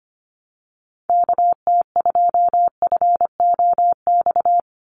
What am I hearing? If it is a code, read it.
Morse code, heard as a keyed tone, about 25 wpm, 690 Hz. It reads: KT2FOX
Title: USA Call Signs 25wpm